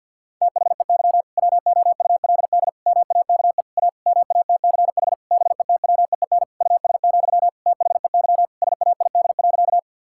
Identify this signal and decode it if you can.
Morse code, heard as a keyed tone, about 50 wpm, 700 Hz. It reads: THEY WOULD MAKE A MATCH BETWEEN US, THEY SAID.
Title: Sentences from Top 1600 Words 50wpm